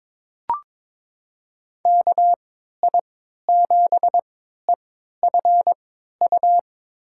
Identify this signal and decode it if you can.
Morse code, heard as a keyed tone, about 22 wpm, spaced as for 13 wpm, 700 Hz. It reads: KI7EFU
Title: USA Call Signs - 6 Characters - 3x Character Spacing - 22wpm